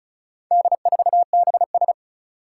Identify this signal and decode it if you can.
Morse code, heard as a keyed tone, about 34 wpm, 700 Hz. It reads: D4BS
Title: Four Challenging Letter-Number Combinations 35wpm